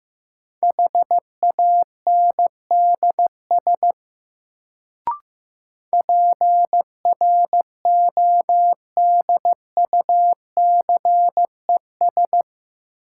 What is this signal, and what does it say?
Morse code, heard as a keyed tone, about 15 wpm, 700 Hz. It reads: HANDS PRODUCES
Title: Sentences from Top 1000 Words - Review Part 6 - New Word Drill - 15wpm